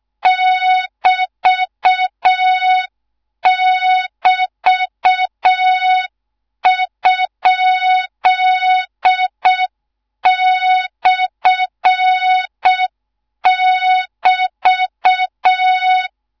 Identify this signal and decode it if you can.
Morse code, heard as a keyed tone, about 6 wpm, 3700 Hz, with noise. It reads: <BT><BT>?/<BT>
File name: Morze-HA2MN / hang / 13IRASJELEK1 HA2MN